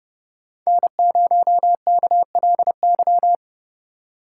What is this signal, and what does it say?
Morse code, heard as a keyed tone, about 33 wpm, 700 Hz. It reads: N0KLY